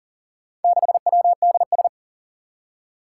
Morse code, heard as a keyed tone, about 37 wpm, 700 Hz. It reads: 6WDS
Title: Four Challenging Letter-Number Combinations 40wpm